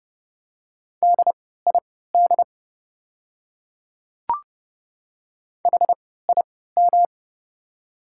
Morse code, heard as a keyed tone, about 30 wpm, spaced as for 17 wpm, 700 Hz. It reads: DID HIM
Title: 3x Character Spacing - Top 100 Words - 30wpm